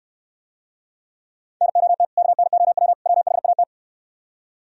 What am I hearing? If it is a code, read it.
Morse code, heard as a keyed tone, about 70 wpm, 700 Hz. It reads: NOT KNOW WHAT